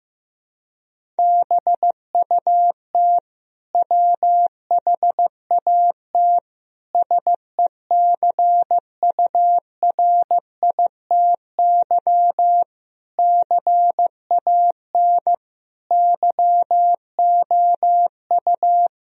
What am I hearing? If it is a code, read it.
Morse code, heard as a keyed tone, about 15 wpm, 700 Hz. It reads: BUT WHAT SECURITY CAN YOU
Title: Sentences from Top 1000 Words - Review Part 4 - 15wpm